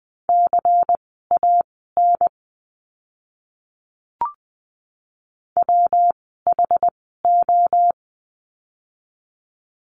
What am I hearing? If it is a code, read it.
Morse code, heard as a keyed tone, about 20 wpm, spaced as for 15 wpm, 700 Hz. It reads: CAN WHO
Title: 160 Lesson 26f - Letter g - Words 20wpm